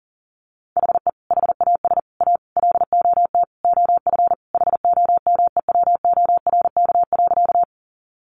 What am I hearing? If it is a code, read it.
Morse code, heard as a keyed tone, about 40 wpm, 700 Hz. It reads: HE HAS A LOT OF HOMEWORK.
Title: Additional Easy Phrases 40wpm